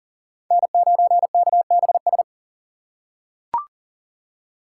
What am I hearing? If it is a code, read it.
Morse code, heard as a keyed tone, about 40 wpm, 700 Hz. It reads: N9KBS
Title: USA Call Signs - 5 Characters - 4x Character Spacing - 40wpm